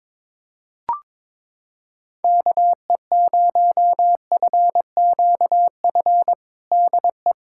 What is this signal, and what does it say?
Morse code, heard as a keyed tone, about 22 wpm, 700 Hz. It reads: KE0FQF DE
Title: USA Callsign DE Callsign - 6 Characters - 22wpm